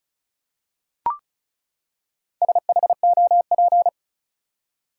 Morse code, heard as a keyed tone, about 36 wpm, 700 Hz. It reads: SHOP
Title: Four Letter Words - 35wpm